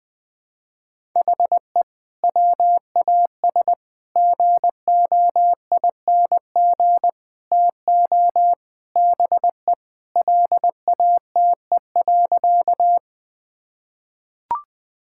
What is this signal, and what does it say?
Morse code, heard as a keyed tone, about 20 wpm, 700 Hz. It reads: HE WAS GOING TO BE LATE.
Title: Sentences with 5-Word Phrases - 20wpm